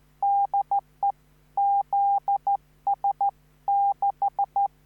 Morse code, heard as a keyed tone, about 14 wpm, 800 Hz. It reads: DE ZS6
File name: QSO05